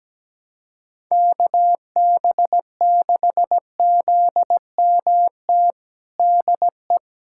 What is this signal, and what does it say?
Morse code, heard as a keyed tone, about 17 wpm, 700 Hz. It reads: KB6ZMT DE
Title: USA Callsign DE Callsign 17wpm